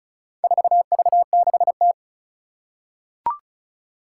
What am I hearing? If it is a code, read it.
Morse code, heard as a keyed tone, about 36 wpm, 700 Hz. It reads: 4V6T
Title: Four Challenging Letter-Number Combinations 35wpm